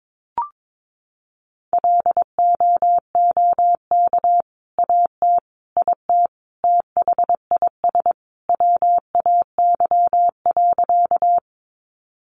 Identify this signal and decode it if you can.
Morse code, heard as a keyed tone, about 22 wpm, 700 Hz. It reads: LOOK AT IT THIS WAY.
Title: Sentences from Top 100 Words 22wpm